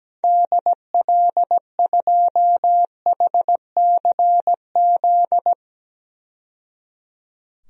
Morse code, heard as a keyed tone, about 17 wpm, 700 Hz. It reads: DL2HCZ
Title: DXCC-100 Call Signs - 6 Characters - 17wpm